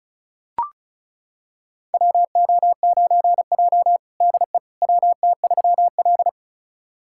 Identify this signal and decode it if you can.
Morse code, heard as a keyed tone, about 34 wpm, 700 Hz. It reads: WO9J DE WT3L
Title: USA Callsign DE Callsign - 4 Characters - 35wpm